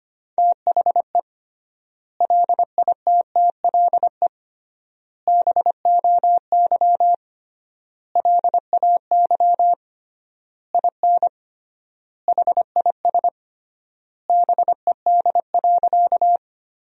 Morse code, heard as a keyed tone, about 25 wpm, 700 Hz. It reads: THE LITTLE BOY LAY IN HIS BED.
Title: 3x Word-Spacing - Sentences from Top 500 Words - 25wpm